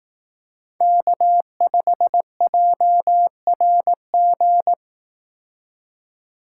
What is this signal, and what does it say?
Morse code, heard as a keyed tone, about 18 wpm, 700 Hz. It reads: K5JRG
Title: Speed Racing - USA 1x3 Call Signs 15wpm